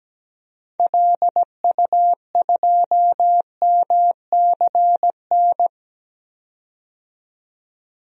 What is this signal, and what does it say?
Morse code, heard as a keyed tone, about 17 wpm, 700 Hz. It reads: LU2MCN
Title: DXCC-100 Call Signs - 6 Characters - 17wpm